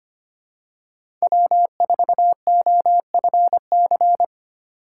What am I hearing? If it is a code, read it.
Morse code, heard as a keyed tone, about 25 wpm, 700 Hz. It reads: W4OFC
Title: USA Call Signs - 6x Character Spacing - 25wpm